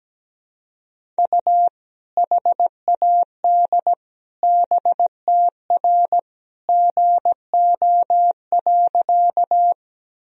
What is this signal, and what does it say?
Morse code, heard as a keyed tone, about 17 wpm, 700 Hz. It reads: U HAD BTR GO.
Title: Sentences from Top 400 Words and Abbreviations 17wpm